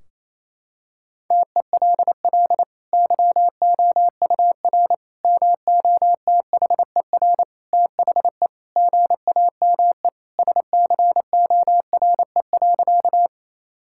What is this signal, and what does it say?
Morse code, heard as a keyed tone, about 28 wpm, 700 Hz. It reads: TELL YOUR MOTHER THE GAME SCORE.
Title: Sentences from Top 1600 Words 28wpm